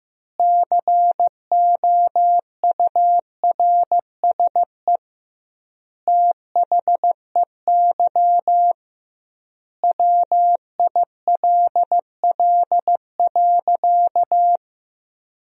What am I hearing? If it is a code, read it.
Morse code, heard as a keyed tone, about 15 wpm, 700 Hz. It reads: COURSE THEY WILL.
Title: 2x Word-Spacing - Sentences from Top 500 Words - 15wpm